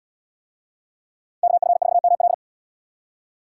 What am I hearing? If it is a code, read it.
Morse code, heard as a keyed tone, about 77 wpm, 690 Hz. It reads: KF4ML